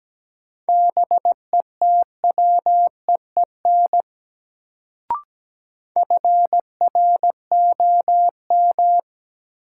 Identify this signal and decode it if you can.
Morse code, heard as a keyed tone, about 17 wpm, 700 Hz. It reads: BETWEEN FROM
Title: Sets of 2 Words - New Word Drill - 17wpm